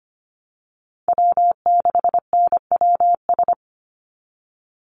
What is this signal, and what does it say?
Morse code, heard as a keyed tone, about 25 wpm, 700 Hz. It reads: W6NWS
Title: Call Signs - 25wpm